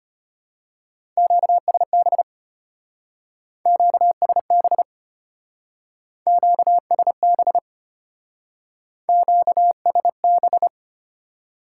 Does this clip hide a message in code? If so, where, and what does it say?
Yes: Morse code from 1.2 to 10.7 s, heard as a keyed tone, about 32 wpm, 700 Hz. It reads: QSB QSB QSB QSB